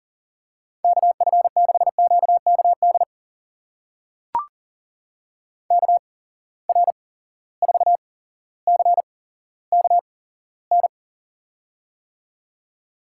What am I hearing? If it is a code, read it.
Morse code, heard as a keyed tone, about 42 wpm, spaced as for 17 wpm, 700 Hz. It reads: KF6QKD KR4CKN
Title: USA Call Signs - 6 Characters - 8x Character Spacing - 40wpm